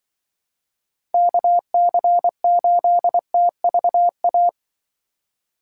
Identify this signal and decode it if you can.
Morse code, heard as a keyed tone, about 24 wpm, 700 Hz. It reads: KC8TVA
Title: Speed Racing - USA Call Signs 20wpm